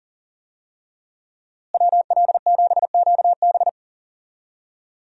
Morse code, heard as a keyed tone, about 40 wpm, 700 Hz. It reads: WL7QB
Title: USA Call Signs - 5 Characters - Repeated 2x - 40wpm